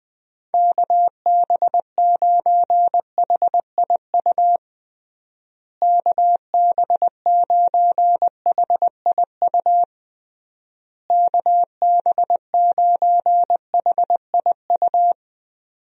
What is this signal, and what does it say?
Morse code, heard as a keyed tone, about 20 wpm, 700 Hz. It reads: KB9HIU KB9HIU KB9HIU
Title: USA Call Signs - 6 Characters - Repeated 3x - 20wpm